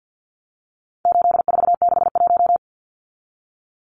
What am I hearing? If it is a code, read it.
Morse code, heard as a keyed tone, about 50 wpm, 700 Hz. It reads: 8461
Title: Numbers - 4-Digit 50wpm